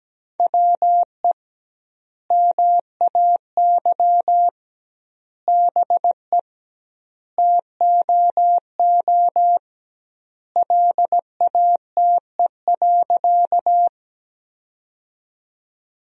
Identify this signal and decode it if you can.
Morse code, heard as a keyed tone, about 17 wpm, 700 Hz. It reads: WE MAY BE TOO LATE.
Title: 2x Word-Spacing - Sentences from Top 300 Words - 17wpm